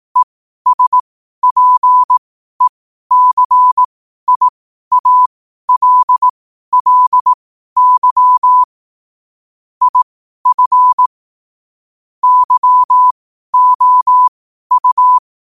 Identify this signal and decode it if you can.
Morse code, heard as a keyed tone, about 18 wpm, spaced as for 12 wpm, 1000 Hz. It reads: ESPECIALLY IF YOU
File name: qotdfiles / 12wpm